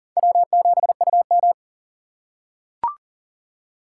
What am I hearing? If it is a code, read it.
Morse code, heard as a keyed tone, about 41 wpm, 700 Hz. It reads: W7UM
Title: USA Call Signs - 4x Character Spacing - 40wpm